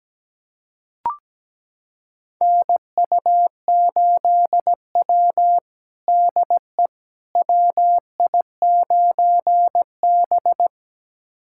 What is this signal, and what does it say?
Morse code, heard as a keyed tone, about 17 wpm, 700 Hz. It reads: NU8W DE WI9B
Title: USA Callsign DE Callsign - 4 Characters - 17wpm